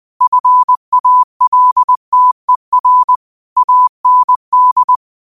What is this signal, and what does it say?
Morse code, heard as a keyed tone, about 20 wpm, 1000 Hz. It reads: FALTER AND
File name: qotdfiles / 20wpm